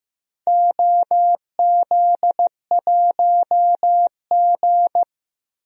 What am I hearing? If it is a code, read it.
Morse code, heard as a keyed tone, about 15 wpm, 700 Hz. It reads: OZ1G
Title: DXCC-100 Call Signs - 4 Characters - 15wpm